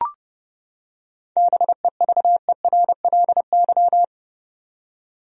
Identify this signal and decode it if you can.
Morse code, heard as a keyed tone, about 30 wpm, 700 Hz. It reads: BEVERLY
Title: USA Common Names - 30wpm